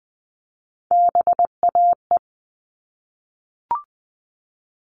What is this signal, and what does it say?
Morse code, heard as a keyed tone, about 20 wpm, 700 Hz. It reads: BAE
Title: All Cut Numbers - 3-Digit 20wpm